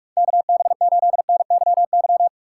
Morse code, heard as a keyed tone, about 44 wpm, 700 Hz. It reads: KB8NYY